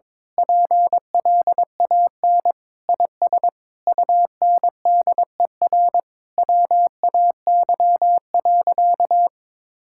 Morse code, heard as a keyed tone, about 21 wpm, 700 Hz. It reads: PLAN IS UNDER WAY.
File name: Sentences with 9-Word Phrases - 22wpm